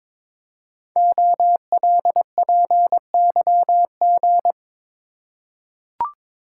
Morse code, heard as a keyed tone, about 22 wpm, 700 Hz. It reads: OLPYG
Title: Code Groups - 5 Letters - 22wpm